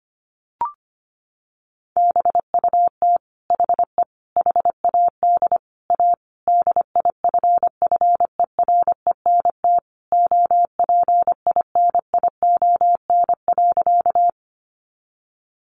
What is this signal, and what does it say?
Morse code, heard as a keyed tone, about 25 wpm, 700 Hz. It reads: BUT HE HAD A DIFFERENT OPINION.
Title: Sentences from Top 1200 Words 25wpm